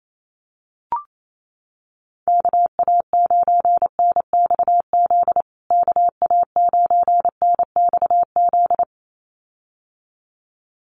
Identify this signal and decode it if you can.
Morse code, heard as a keyed tone, about 28 wpm, 700 Hz. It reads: KA9NXZ KA9NXZ
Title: USA Call Signs Repeated 2x 28wpm